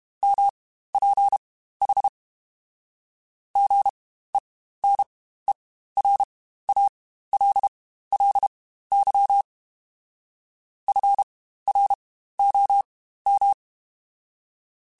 Morse code, heard as a keyed tone, about 32 wpm, spaced as for 15 wpm, 800 Hz. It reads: MPH GENERALLY FROM
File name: Spaced Characters / Spaced Characters-mp3s / Weather 4 14 sp @30